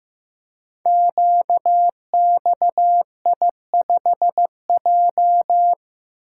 Morse code, heard as a keyed tone, about 15 wpm, 700 Hz. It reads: QXI5J
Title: Code Groups - 5 Letter:Numbers - 15wpm